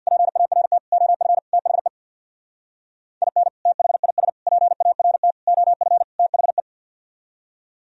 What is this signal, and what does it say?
Morse code, heard as a keyed tone, about 61 wpm, 700 Hz. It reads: PART OF THE IN THIS PART OF THE
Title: Speed-Racing - Sets of 5 Words 50wpm